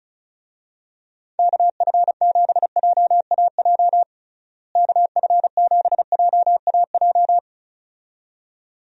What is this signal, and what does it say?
Morse code, heard as a keyed tone, about 34 wpm, 700 Hz. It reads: KF7JAJ KF7JAJ
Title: USA Call Signs - 6 Characters - Repeated 2x - 35wpm